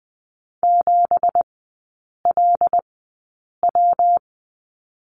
Morse code, heard as a keyed tone, about 20 wpm, spaced as for 8 wpm, 690 Hz. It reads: 7LW
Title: 172 Lesson 28e - Number 9 - Three Characters 20wpm